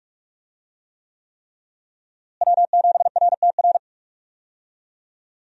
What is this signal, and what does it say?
Morse code, heard as a keyed tone, about 45 wpm, 700 Hz. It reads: W7RTR
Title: USA Call Signs - 5 Characters - Repeated 3x - 45wpm